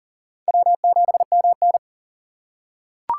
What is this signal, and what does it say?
Morse code, heard as a keyed tone, about 40 wpm, 700 Hz. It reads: W7MN
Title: USA Call Signs - 3x Character Spacing - 40wpm